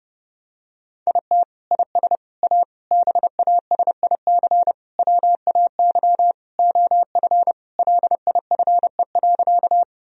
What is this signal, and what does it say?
Morse code, heard as a keyed tone, about 30 wpm, 700 Hz. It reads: IT IS A BASIC WAY OF LIFE.
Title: Sentences from Top 1000 Words - Review Part 3 - 30wpm